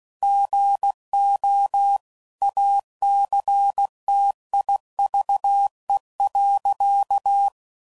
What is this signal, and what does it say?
Morse code, heard as a keyed tone, about 16 wpm, 800 Hz. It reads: GO ACTIVE.
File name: CW Code Groups @15WPM / Technical Subjects / OR Gate